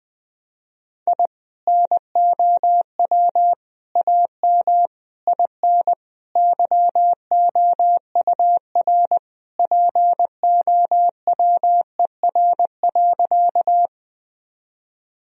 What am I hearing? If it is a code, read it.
Morse code, heard as a keyed tone, about 21 wpm, 700 Hz. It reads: I NOW AM IN YOUR POWER.
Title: Sentences from Top 1000 Words - Review Part 2 - 20wpm